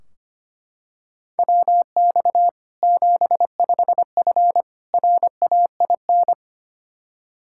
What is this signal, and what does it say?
Morse code, heard as a keyed tone, about 25 wpm, 700 Hz. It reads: WX 75F RAIN
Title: Practice QSO - Exchange 2 - Info - 25wpm